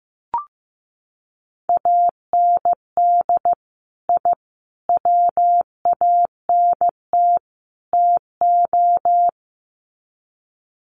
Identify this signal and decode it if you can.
Morse code, heard as a keyed tone, about 15 wpm, 700 Hz. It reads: AND I WANT TO
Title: Sets of 4 Words 15wpm